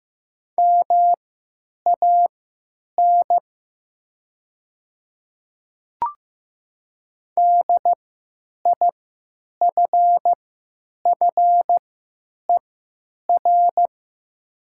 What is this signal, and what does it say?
Morse code, heard as a keyed tone, about 15 wpm, spaced as for 8 wpm, 700 Hz. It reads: MAN DIFFER
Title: 3x Character Spacing - Top 200 Words - 15wpm